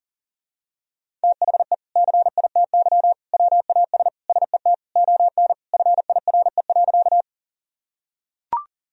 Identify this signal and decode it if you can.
Morse code, heard as a keyed tone, about 40 wpm, 700 Hz. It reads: THE CITY WAS SET ON FIRE.
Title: Sentences from Top 1000 Words - Review Part 1 - 40wpm